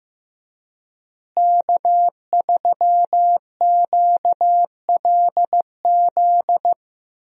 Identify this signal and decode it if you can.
Morse code, heard as a keyed tone, about 15 wpm, 700 Hz. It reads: K3QLZ
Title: USA Call Signs - 5 Characters - Repeated 2x - 15wpm